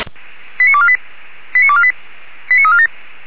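Five-tone selective call (ZVEI), heard as quick runs of five tones, tones of about 70 ms. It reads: 89248 89248 89347
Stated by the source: ZVEI five-tone call 89406, received on SDR.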